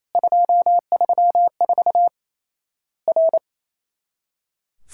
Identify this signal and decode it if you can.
Morse code, heard as a keyed tone, about 28 wpm, 700 Hz. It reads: 234 R
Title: Worked All Europe DX Contest - QTC Exchange - 28wpm